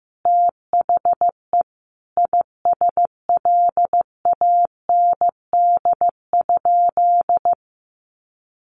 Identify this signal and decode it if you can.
Morse code, heard as a keyed tone, about 14 wpm, 700 Hz. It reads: THE ISLAND?
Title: Sentences from Top 1100 Words 15wpm